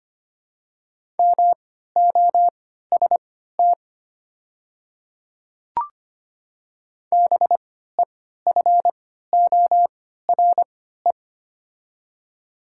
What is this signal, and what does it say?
Morse code, heard as a keyed tone, about 25 wpm, spaced as for 14 wpm, 690 Hz. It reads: MOST BEFORE
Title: 3x Character Spacing - Top 200 Words - 25wpm